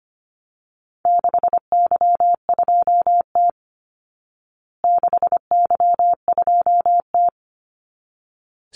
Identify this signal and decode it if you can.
Morse code, heard as a keyed tone, about 25 wpm, 700 Hz. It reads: 6Y2T 6Y2T
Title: Call Signs Repeated 3x - 25wpm